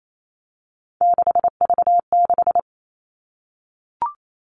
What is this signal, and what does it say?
Morse code, heard as a keyed tone, about 28 wpm, 700 Hz. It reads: 6V6
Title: Three Challenging Letter-Number Combinations 28wpm